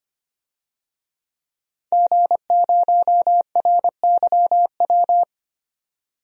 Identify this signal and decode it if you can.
Morse code, heard as a keyed tone, about 25 wpm, 700 Hz. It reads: G0RYW
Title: DXCC-100 Call Signs - 5 Characters - 25wpm